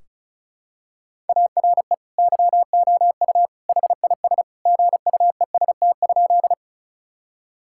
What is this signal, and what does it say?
Morse code, heard as a keyed tone, about 36 wpm, 700 Hz. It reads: ARE YOU HIS GUEST?
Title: Sentences from Top 1600 Words 35wpm